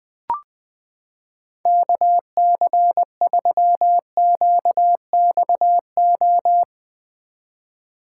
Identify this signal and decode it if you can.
Morse code, heard as a keyed tone, about 21 wpm, 700 Hz. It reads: KC3QXO
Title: 2023 POTA Activator Call Signs - 20wpm